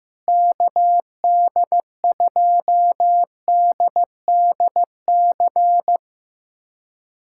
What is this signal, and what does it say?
Morse code, heard as a keyed tone, about 15 wpm, 700 Hz. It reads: KD2DDC
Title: USA Callsign DE Callsign 15wpm